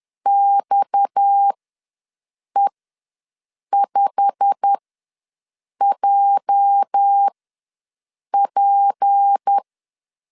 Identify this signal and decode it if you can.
Morse code, heard as a keyed tone, about 10 wpm, spaced as for 6 wpm, 800 Hz. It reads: XE5JP
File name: LEZIONI / 8 YJ45 / 8 YJ45 ws 10wpm 3843